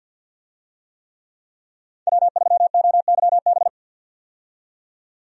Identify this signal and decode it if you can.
Morse code, heard as a keyed tone, about 51 wpm, 700 Hz. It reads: W3OYB
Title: USA Call Signs - 5 Characters - 50wpm